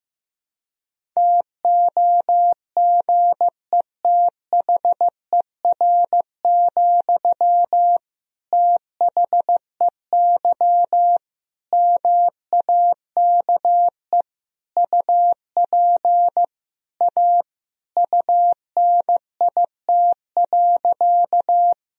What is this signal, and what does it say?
Morse code, heard as a keyed tone, about 15 wpm, 700 Hz. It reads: TOGETHER, THEY MAKE UP A UNIT.
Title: Sentences from Top 1000 Words - Review Part 2 - 15wpm